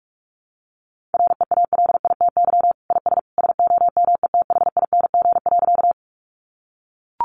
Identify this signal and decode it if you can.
Morse code, heard as a keyed tone, about 44 wpm, 700 Hz. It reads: REALITY IS SOMETHING.
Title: Sentences from Top 1200 Words 45wpm